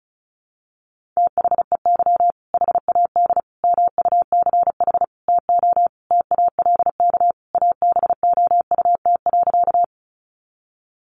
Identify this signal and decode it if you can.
Morse code, heard as a keyed tone, about 34 wpm, 700 Hz. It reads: THEY HAD MUCH TO TALK ABOUT.
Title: Sentences from Top 400 Words 35wpm